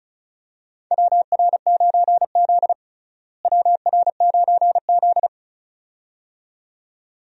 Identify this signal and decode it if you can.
Morse code, heard as a keyed tone, about 36 wpm, 700 Hz. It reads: WR9Z WR9Z
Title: USA Call Signs - 4 Characters - Repeated 2x - 35wpm